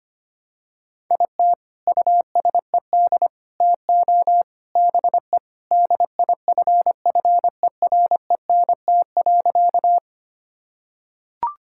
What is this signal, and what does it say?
Morse code, heard as a keyed tone, about 25 wpm, 700 Hz. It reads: IT USED TO BE DIFFERENT.
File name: practice / 25wpm / Sentences with 4-Word Phrases - 25wpm